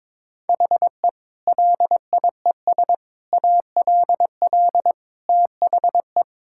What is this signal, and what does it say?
Morse code, heard as a keyed tone, about 21 wpm, 700 Hz. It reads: HE LIES ALL THE
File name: practice / 22wpm / Sentences with 3-Word Phrases - 22wpm